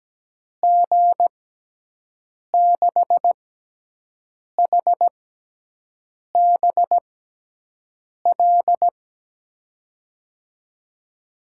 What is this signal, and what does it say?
Morse code, heard as a keyed tone, about 17 wpm, spaced as for 6 wpm, 700 Hz. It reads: G6HBL
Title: USA Call Signs - 6x Character Spacing - 17wpm